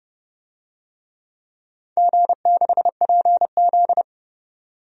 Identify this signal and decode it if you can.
Morse code, heard as a keyed tone, about 28 wpm, 700 Hz. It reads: G6PZ
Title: DXCC-100 Call Signs - 4 Characters - 30wpm